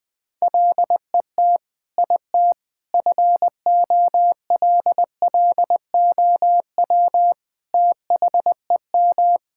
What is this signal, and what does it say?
Morse code, heard as a keyed tone, about 20 wpm, 700 Hz. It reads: LET IT FOLLOW THEM
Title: Sentences from Top 1600 Words 20wpm